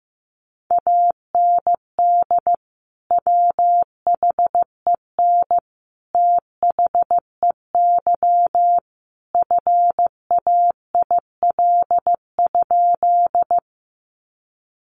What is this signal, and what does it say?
Morse code, heard as a keyed tone, about 15 wpm, 700 Hz. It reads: AND WHEN THEY FAIL?
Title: Sentences from Top 1100 Words 15wpm